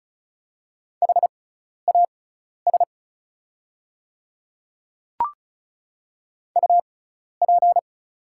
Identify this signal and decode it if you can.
Morse code, heard as a keyed tone, about 36 wpm, spaced as for 12 wpm, 700 Hz. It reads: HAS UP
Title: 6x Character Spacing - Top 300 Words - 35wpm